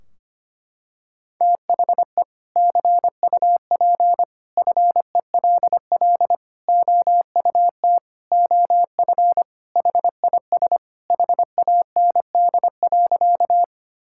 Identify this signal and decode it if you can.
Morse code, heard as a keyed tone, about 25 wpm, 700 Hz. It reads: THE CUP FELL OUT OF HIS HAND.
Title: Sentences from Top 1600 Words 25wpm